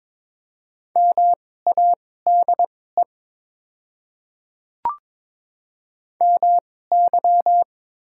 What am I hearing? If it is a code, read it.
Morse code, heard as a keyed tone, about 22 wpm, spaced as for 16 wpm, 700 Hz. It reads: MADE MY
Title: 2x Character Spacing - Top 200 Words - 22wpm